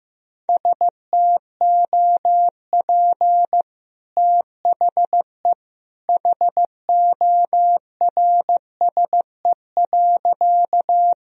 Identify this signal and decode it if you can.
Morse code, heard as a keyed tone, about 15 wpm, 700 Hz. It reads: STOP THE HORSE.
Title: Sentences from Top 1000 Words - Review Part 1 - 15wpm